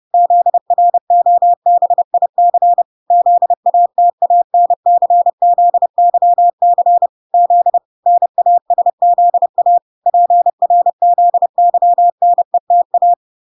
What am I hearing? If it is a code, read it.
Morse code, heard as a keyed tone, about 30 wpm, 700 Hz. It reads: ZROBIC ZATANCZYC Z NASZA PRZYNETA